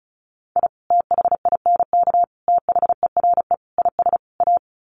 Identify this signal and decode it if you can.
Morse code, heard as a keyed tone, about 34 wpm, 700 Hz. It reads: I THINK THERE IS A
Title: Sets of 5 Words 35wpm